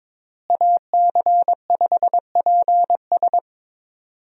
Morse code, heard as a keyed tone, about 22 wpm, 700 Hz. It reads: AC5PS
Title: USA Call Signs - 5 Characters - Repeated 3x - 22wpm